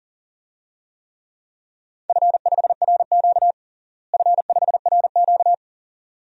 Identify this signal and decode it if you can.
Morse code, heard as a keyed tone, about 40 wpm, 690 Hz. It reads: F5RQ F5RQ
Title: DXCC-100 Call Signs - 4 Characters - Repeated 2x - 40wpm